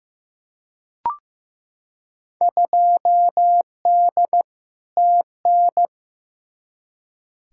Morse code, heard as a keyed tone, about 15 wpm, 700 Hz. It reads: 2D TN
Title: Field Day Exchanges Encore - 15wpm